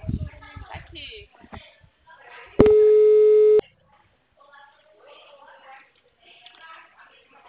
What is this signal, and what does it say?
Ringback tone (Europe, 425 Hz), with speech.
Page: Itatiba